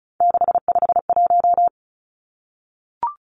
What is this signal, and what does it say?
Morse code, heard as a keyed tone, about 34 wpm, 700 Hz. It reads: 651
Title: Numbers - 3-Digit 35wpm